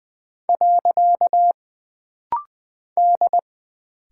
Morse code, heard as a keyed tone, about 20 wpm, 700 Hz. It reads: . D
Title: Single Character (Letter:Number:Punctuation:Prosign) - Rapid-Fire - 20wpm